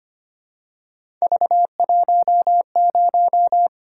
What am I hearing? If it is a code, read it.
Morse code, heard as a keyed tone, about 25 wpm, 700 Hz. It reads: V10
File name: Three Letter-Number Combinations - 25wpm